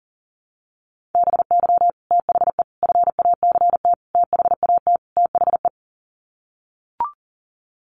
Morse code, heard as a keyed tone, about 40 wpm, 700 Hz. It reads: BY THE FACT THAT THE
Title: Sets of 5 Words Encore 40wpm